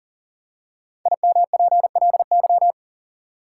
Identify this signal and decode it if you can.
Morse code, heard as a keyed tone, about 40 wpm, 700 Hz. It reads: IMPLY
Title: Five Letter Words - 40wpm